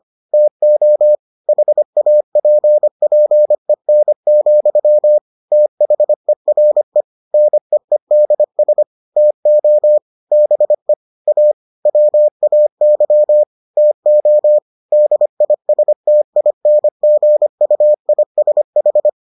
Morse code, heard as a keyed tone, about 25 wpm, 600 Hz. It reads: TO HAPPEN, THERE NEEDS TO BE A WAY TO DISTINGUISH